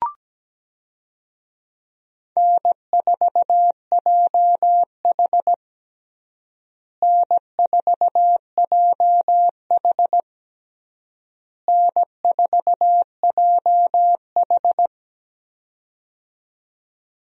Morse code, heard as a keyed tone, about 17 wpm, 700 Hz. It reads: N4JH N4JH N4JH
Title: USA Call Signs - 4 Characters - Repeated 3x - 17wpm